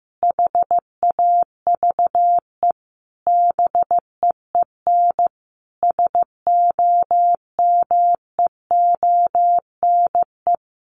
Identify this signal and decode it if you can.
Morse code, heard as a keyed tone, about 15 wpm, 700 Hz. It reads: HAVE BEEN SOMEONE
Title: Sentences from Top 800 Words 15wpm